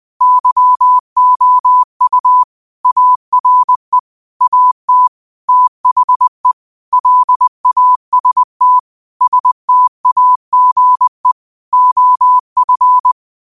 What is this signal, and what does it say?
Morse code, heard as a keyed tone, about 20 wpm, 1000 Hz. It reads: YOU ARE AT THE LAST STAGE OF
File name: qotdfiles / 20wpm